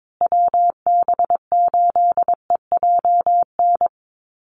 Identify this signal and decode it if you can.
Morse code, heard as a keyed tone, about 22 wpm, 700 Hz. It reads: WB8EJN
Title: Call Signs Repeated 3x - 22wpm